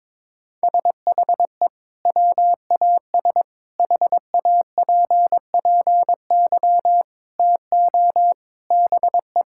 Morse code, heard as a keyed tone, about 22 wpm, 700 Hz. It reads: SHE WAS HAPPY TO BE
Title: Sentences from Top 1000 Words - Review Part 5 - 22wpm